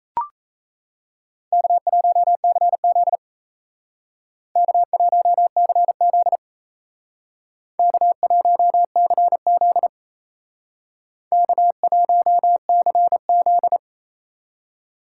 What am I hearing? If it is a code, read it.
Morse code, heard as a keyed tone, about 35 wpm, 700 Hz. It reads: K1CZ K1CZ K1CZ K1CZ
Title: Speed Racing - USA 1x2 Call Signs 28wpm